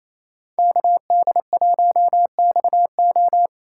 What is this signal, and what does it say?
Morse code, heard as a keyed tone, about 36 wpm, 700 Hz. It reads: KD1XO